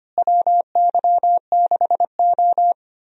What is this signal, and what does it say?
Morse code, heard as a keyed tone, about 25 wpm, 700 Hz. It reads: WY6O
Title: USA Call Signs - 4 Characters - Repeated 2x - 25wpm